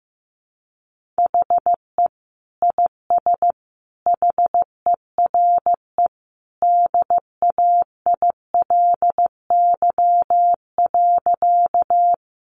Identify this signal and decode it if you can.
Morse code, heard as a keyed tone, about 15 wpm, 700 Hz. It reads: HE IS HERE DAILY.
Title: Sentences from Top 1100 Words 15wpm